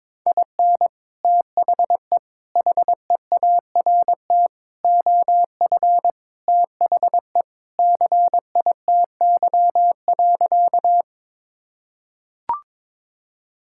Morse code, heard as a keyed tone, about 22 wpm, 700 Hz. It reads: IN THE HEART OF THE CITY.
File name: practice / 22wpm / Sentences with 3-Word Phrases - 22wpm